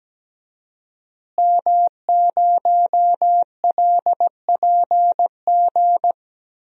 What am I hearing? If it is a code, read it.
Morse code, heard as a keyed tone, about 17 wpm, 700 Hz. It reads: M0LPG
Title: DXCC-100 Call Signs - 5 Characters - Repeated 2x - 17wpm